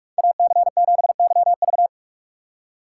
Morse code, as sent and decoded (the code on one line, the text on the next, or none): .- -.-. --... -.-- ...-
AC7YV